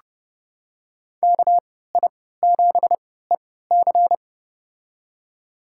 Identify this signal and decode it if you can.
Morse code, heard as a keyed tone, about 32 wpm, spaced as for 17 wpm, 700 Hz. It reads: KI7EC
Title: USA Call Signs - 3x Character Spacing - 30wpm